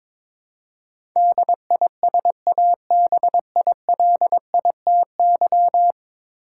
Morse code, heard as a keyed tone, about 22 wpm, 700 Hz. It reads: DISABILITY
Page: morse-code-ninja-practice-22wpm